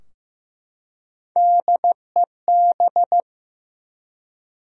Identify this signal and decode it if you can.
Morse code, heard as a keyed tone, about 15 wpm, 700 Hz. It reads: DEB
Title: USA Common Names - Max 3 Letters - 15wpm